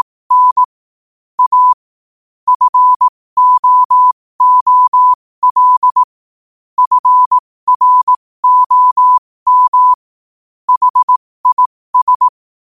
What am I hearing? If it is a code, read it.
Morse code, heard as a keyed tone, about 18 wpm, spaced as for 15 wpm, 1000 Hz. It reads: N A FOOL FROM HIS